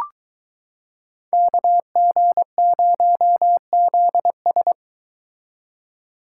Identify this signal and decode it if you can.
Morse code, heard as a keyed tone, about 23 wpm, 700 Hz. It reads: KG0ZS